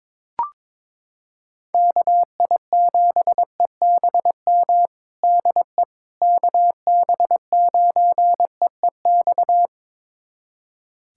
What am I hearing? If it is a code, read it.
Morse code, heard as a keyed tone, about 21 wpm, 700 Hz. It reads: KI7EBM DE KB9EEX
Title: USA Callsign DE Callsign - 6 Characters - 22wpm